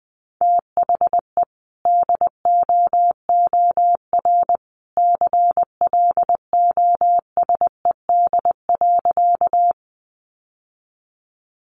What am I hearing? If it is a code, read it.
Morse code, heard as a keyed tone, about 19 wpm, 700 Hz. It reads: THE DOOR CLOSED.
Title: Sentences from Top 1400 Words 20wpm